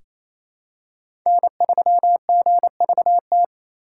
Morse code, heard as a keyed tone, about 28 wpm, 700 Hz. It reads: N3GVT
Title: USA Call Signs - 5 Characters - 2x Character Spacing - 28wpm